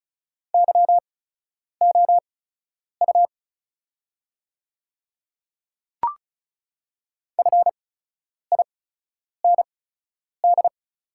Morse code, heard as a keyed tone, about 35 wpm, spaced as for 10 wpm, 700 Hz. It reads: YOU FIND
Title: 8x Character Spacing - Top 100 Words - 35wpm